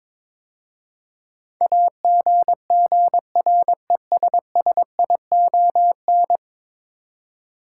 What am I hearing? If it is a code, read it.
Morse code, heard as a keyed tone, about 22 wpm, 700 Hz. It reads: AGGRESSION
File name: practice / 22wpm / 10 Letter Words - First 3 Letters Given - 22wpm